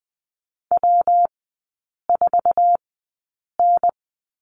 Morse code, heard as a keyed tone, about 20 wpm, spaced as for 8 wpm, 690 Hz. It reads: W4N
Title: 198 Lesson 32e - Letter k - Three Characters 20wpm